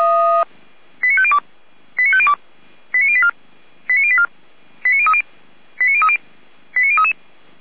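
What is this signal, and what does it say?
Five-tone selective call (ZVEI), heard as quick runs of five tones, tones of about 70 ms. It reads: 89502 89502 89084 89084 89030 89030 89033
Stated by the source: ZVEI five-tone call received by SDR.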